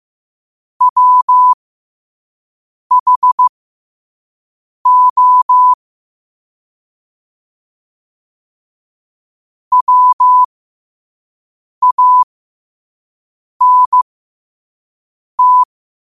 Morse code, heard as a keyed tone, about 15 wpm, spaced as for 5 wpm, 1000 Hz. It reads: WHO WANT